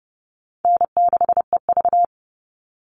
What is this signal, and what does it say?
Morse code, heard as a keyed tone, about 30 wpm, 700 Hz. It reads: N6EV
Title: Call Signs Repeated 3x - 30wpm